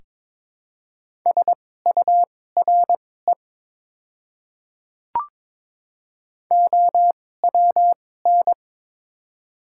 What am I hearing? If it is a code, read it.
Morse code, heard as a keyed tone, about 23 wpm, spaced as for 16 wpm, 700 Hz. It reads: SURE OWN
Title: 2x Character Spacing - Top 300 Words - 22wpm